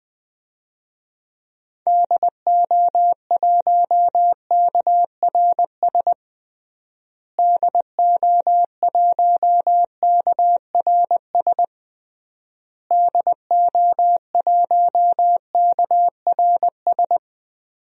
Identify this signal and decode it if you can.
Morse code, heard as a keyed tone, about 20 wpm, 700 Hz. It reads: DO1KRS DO1KRS DO1KRS
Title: DXCC-100 Call Signs - 6 Characters - Repeated 3x - 20wpm